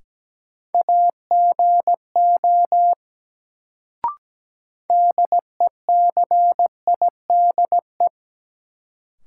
Morse code, heard as a keyed tone, about 17 wpm, 700 Hz. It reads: AGO DECIDE
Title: Sentences from Top 500 Words - New Word Drill - 17wpm